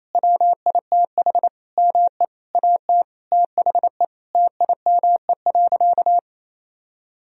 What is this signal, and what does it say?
Morse code, heard as a keyed tone, about 28 wpm, 700 Hz. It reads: WITH ME AT THE TIME.
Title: Sentences from Top 1000 Words - Review Part 4 - 28wpm